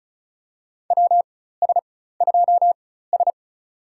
Morse code, heard as a keyed tone, about 34 wpm, spaced as for 16 wpm, 700 Hz. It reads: WS2S
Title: USA Call Signs - 4 Characters - 4x Character Spacing - 35wpm